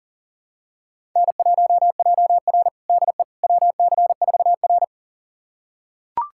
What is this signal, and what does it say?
Morse code, heard as a keyed tone, about 40 wpm, 700 Hz. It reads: N1JR DE WC4R
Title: USA Callsign DE Callsign - 4 Characters - 40wpm